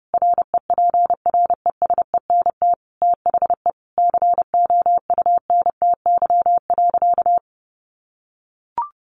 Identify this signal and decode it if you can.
Morse code, heard as a keyed tone, about 30 wpm, 700 Hz. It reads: REPRESENT THE COUNTY.
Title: Sentences from Top 1200 Words 30wpm